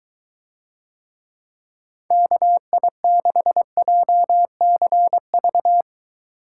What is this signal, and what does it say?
Morse code, heard as a keyed tone, about 22 wpm, 700 Hz. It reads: KI6JCV